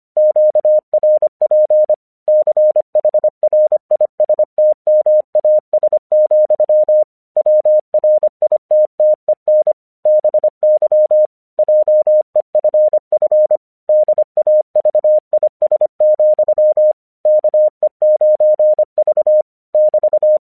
Morse code, heard as a keyed tone, about 25 wpm, 610 Hz. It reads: QRP CHRISTMAS, WRITTEN BY JEFF DAVIS, KE9V <BT>
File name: 1 English 25wpm / AQC01